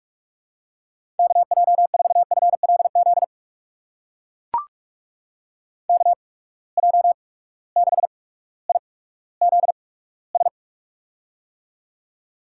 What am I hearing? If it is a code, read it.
Morse code, heard as a keyed tone, about 44 wpm, spaced as for 19 wpm, 700 Hz. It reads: KJ4FLZ KJ6IZS
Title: USA Call Signs - 6 Characters - 8x Character Spacing - 45wpm